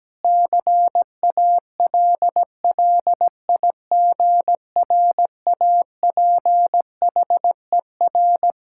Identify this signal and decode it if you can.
Morse code, heard as a keyed tone, about 17 wpm, 700 Hz. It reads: CALLIGRAPHER
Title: Pangram Challenge v2 - 17wpm